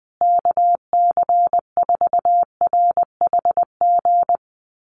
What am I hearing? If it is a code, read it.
Morse code, heard as a keyed tone, about 20 wpm, 700 Hz. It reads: KC4RHG
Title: USA Call Signs Repeated 2x 20wpm